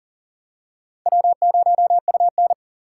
Morse code, heard as a keyed tone, about 40 wpm, 700 Hz. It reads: W0UN
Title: USA Call Signs - 4 Characters - 4x Character Spacing - 40wpm